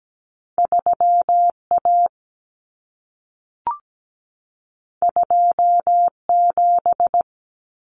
Morse code, heard as a keyed tone, about 17 wpm, 700 Hz. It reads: 3A 27
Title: Common Cut Numbers - 2-Digit 17wpm